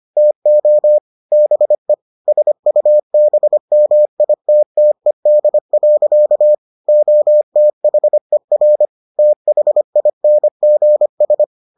Morse code, heard as a keyed tone, about 25 wpm, 600 Hz. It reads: TO BE SUBMITTED. OTHER THINGS